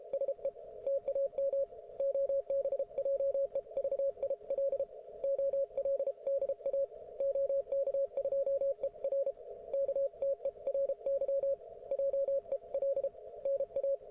Morse code, heard as a keyed tone, about 33 wpm, 550 Hz, with radio noise. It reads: SE TAM OBJEVIL OLDA OK2ER KTERY JEL NA